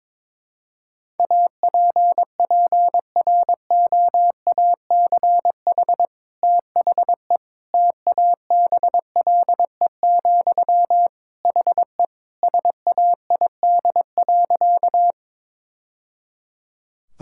Morse code, heard as a keyed tone, about 22 wpm, 700 Hz. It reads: APPROACH THE TABLE, HE SAID.
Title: Sentences from Top 1000 Words - Review Part 6 - 22wpm